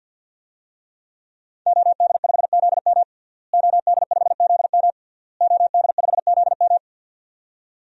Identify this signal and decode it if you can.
Morse code, heard as a keyed tone, about 52 wpm, 700 Hz. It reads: OD5ZM OD5ZM OD5ZM
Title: DXCC-100 Call Signs - 5 Characters - Repeated 3x - 50wpm